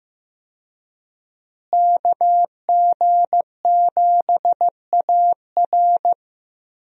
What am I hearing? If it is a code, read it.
Morse code, heard as a keyed tone, about 15 wpm, 700 Hz. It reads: KG7AR
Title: USA Call Signs - 5 Characters - Repeated 3x - 15wpm